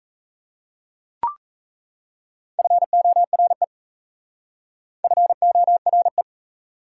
Morse code, heard as a keyed tone, about 36 wpm, 700 Hz. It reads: FORE FORE